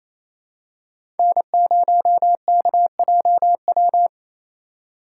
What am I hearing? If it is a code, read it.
Morse code, heard as a keyed tone, about 28 wpm, 700 Hz. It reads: N0KJW
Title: USA Call Signs - 5 Characters - 4x Character Spacing - 28wpm